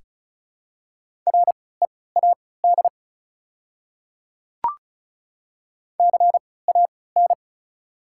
Morse code, heard as a keyed tone, about 36 wpm, spaced as for 20 wpm, 690 Hz. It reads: READ CAN
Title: 3x Character Spacing - Top 200 Words - 35wpm